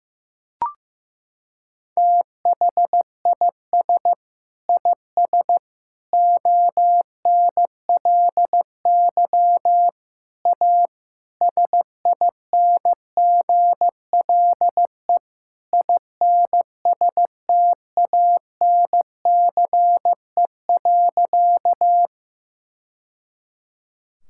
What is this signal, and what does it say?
Morse code, heard as a keyed tone, about 15 wpm, 700 Hz. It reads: THIS IS ONLY A SINGLE INSTANCE.
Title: Sentences from Top 1000 Words - Review Part 5 - 15wpm